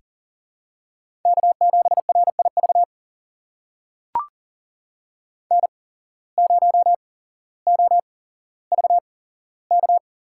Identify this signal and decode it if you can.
Morse code, heard as a keyed tone, about 44 wpm, spaced as for 17 wpm, 700 Hz. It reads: K7RIV N0OVK
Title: USA Call Signs - 5 Characters - 8x Character Spacing - 40wpm